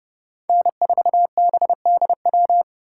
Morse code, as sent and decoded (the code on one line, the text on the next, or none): -. ....- -... -.. .--
N4BDW